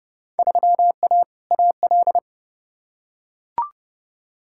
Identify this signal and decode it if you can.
Morse code, heard as a keyed tone, about 32 wpm, 700 Hz. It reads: 3A AL
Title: Field Day Exchanges - 30wpm